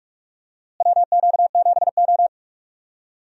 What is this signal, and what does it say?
Morse code, heard as a keyed tone, about 42 wpm, 700 Hz. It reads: WQ7O
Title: USA Call Signs - 4 Characters - Repeated 2x - 45wpm